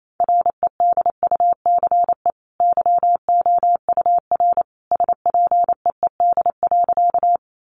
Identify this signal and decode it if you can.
Morse code, heard as a keyed tone, about 28 wpm, 700 Hz. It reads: REDUCE YOUR SPEED.
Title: Sentences from Top 1100 Words 28wpm